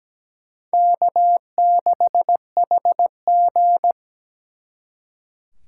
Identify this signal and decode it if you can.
Morse code, heard as a keyed tone, about 20 wpm, 700 Hz. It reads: K6HG
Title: Speed Racing - USA 1x2 Call Signs 17wpm